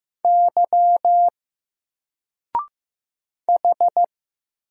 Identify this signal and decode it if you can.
Morse code, heard as a keyed tone, about 15 wpm, 700 Hz. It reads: Y H
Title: Single Letter-Number 15wpm